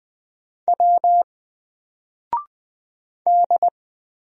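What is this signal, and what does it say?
Morse code, heard as a keyed tone, about 20 wpm, 700 Hz. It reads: W D
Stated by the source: Single letter and number Morse drill at 20 wpm, from N4LCD.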